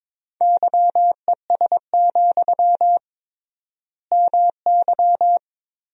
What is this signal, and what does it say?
Morse code, heard as a keyed tone, about 22 wpm, 700 Hz. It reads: YES, MY
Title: 3x Word-Spacing - Sentences from Top 500 Words - 22wpm